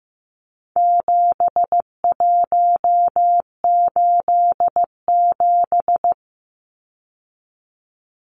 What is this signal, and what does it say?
Morse code, heard as a keyed tone, about 15 wpm, 700 Hz. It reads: 7187
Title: Numbers - 4-Digit 15wpm